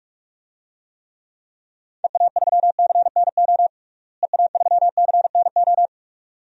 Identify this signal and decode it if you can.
Morse code, heard as a keyed tone, about 43 wpm, 700 Hz. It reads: EA3CNO EA3CNO
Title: DXCC-100 Call Signs - 6 Characters - Repeated 2x - 45wpm